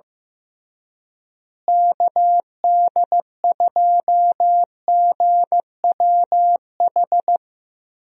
Morse code, heard as a keyed tone, about 15 wpm, 700 Hz. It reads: KD2GWH
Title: USA Call Signs - 6 Characters - Repeated 3x - 15wpm